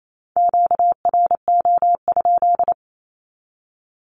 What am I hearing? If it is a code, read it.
Morse code, heard as a keyed tone, about 28 wpm, 700 Hz. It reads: QRO?
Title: Q-Codes 28wpm